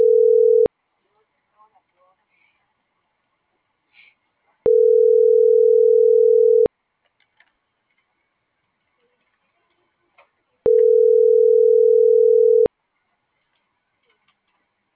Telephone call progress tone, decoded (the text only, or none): ringback tone